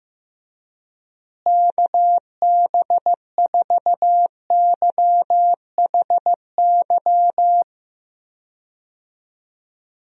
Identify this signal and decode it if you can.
Morse code, heard as a keyed tone, about 15 wpm, 700 Hz. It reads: KB4YHY
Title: USA Call Signs - 6 Characters - Repeated 3x - 15wpm